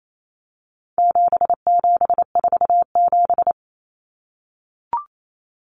Morse code, heard as a keyed tone, about 28 wpm, 700 Hz. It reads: 7747
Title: Common Cut Numbers - 4-Digit 28wpm